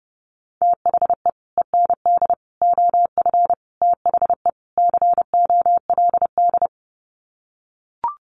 Morse code, heard as a keyed tone, about 30 wpm, 700 Hz. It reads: THE END OF THE COLD
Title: Sets of 5 Words 30wpm